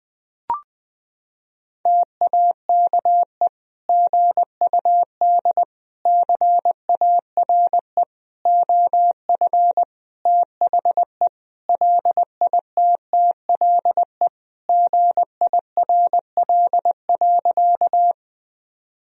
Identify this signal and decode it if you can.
Morse code, heard as a keyed tone, about 20 wpm, 700 Hz. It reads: TAKE GUD CARE OF THE LITTLE GIRL.
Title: Sentences from Top 400 Words and Abbreviations 20wpm